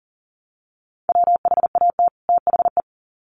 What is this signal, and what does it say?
Morse code, heard as a keyed tone, about 40 wpm, 700 Hz. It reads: WHAT THE
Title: Sets of 2 Words 40wpm